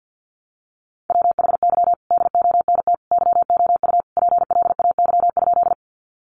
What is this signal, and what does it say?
Morse code, heard as a keyed tone, about 50 wpm, 700 Hz. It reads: WHY DONT YOU PLAY?
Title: Sentences from Top 300 Words 50wpm